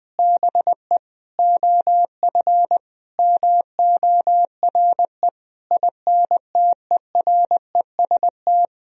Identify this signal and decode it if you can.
Morse code, heard as a keyed tone, about 20 wpm, 700 Hz. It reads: BE OF MORE INTEREST
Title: Sentences from Top 400 Words and Abbreviations 20wpm